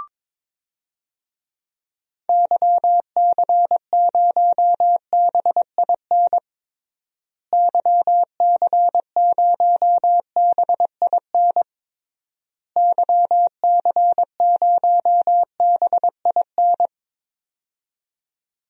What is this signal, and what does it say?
Morse code, heard as a keyed tone, about 21 wpm, 700 Hz. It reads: YC0BIN YC0BIN YC0BIN
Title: DXCC-100 Call Signs - 6 Characters - Repeated 3x - 22wpm